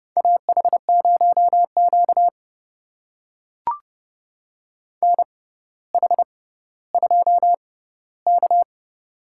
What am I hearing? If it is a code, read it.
Morse code, heard as a keyed tone, about 30 wpm, spaced as for 15 wpm, 700 Hz. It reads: AH0Q NH2K
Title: USA Call Signs - 4 Characters - 6x Character Spacing - 30wpm